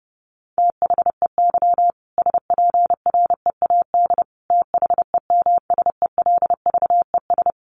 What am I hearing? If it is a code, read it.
Morse code, heard as a keyed tone, about 30 wpm, 700 Hz. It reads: THEY SPREAD THEMSELVES
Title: Sentences from Top 1200 Words 30wpm